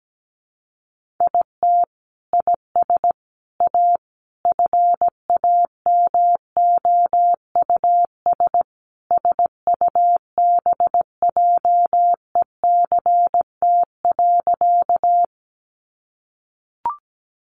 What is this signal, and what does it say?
Morse code, heard as a keyed tone, about 17 wpm, 700 Hz. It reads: IT IS A FAMOUS SUBJECT.
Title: Sentences from Top 1500 Words 17wpm